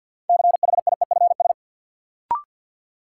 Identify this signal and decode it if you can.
Morse code, heard as a keyed tone, about 53 wpm, 700 Hz. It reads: CHIEFS
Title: Sets of 5 Words Encore - New Word Drill - 50wpm